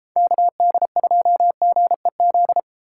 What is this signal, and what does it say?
Morse code, heard as a keyed tone, about 32 wpm, 700 Hz. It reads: KD2GEZ